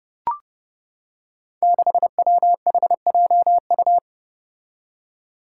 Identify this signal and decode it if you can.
Morse code, heard as a keyed tone, about 30 wpm, 700 Hz. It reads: 6WHJU
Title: Five Challenging Letter-Number Combinations 30wpm